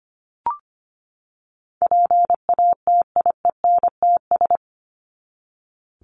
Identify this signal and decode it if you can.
Morse code, heard as a keyed tone, about 25 wpm, 700 Hz. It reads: PATIENTS
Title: ES:S Suffix - Top 100 Words with Base Words 25wpm